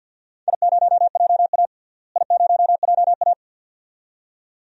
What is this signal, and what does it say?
Morse code, heard as a keyed tone, about 54 wpm, 700 Hz. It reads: I0JA I0JA
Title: DXCC-100 Call Signs - 4 Characters - Repeated 2x - 50wpm